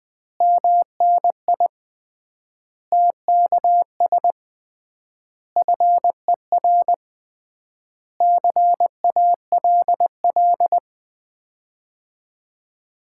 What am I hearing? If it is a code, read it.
Morse code, heard as a keyed tone, about 20 wpm, 700 Hz. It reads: MNI TKS FER CALL
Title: Common QSO Phrases - 3x Word-Spacing - 20wpm